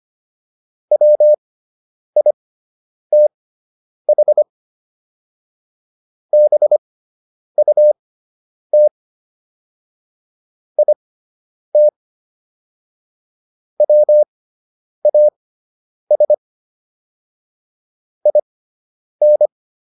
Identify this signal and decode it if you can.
Morse code, heard as a keyed tone, about 25 wpm, spaced as for 9 wpm, 600 Hz. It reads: WITH BUT IT WAS IN